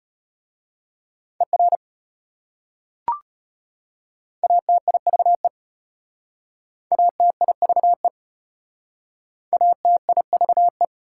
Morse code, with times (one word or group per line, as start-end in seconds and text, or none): 1.4-1.8 ER
4.4-5.5 ATIVE
6.9-8.1 ATIVE
9.5-10.8 ATIVE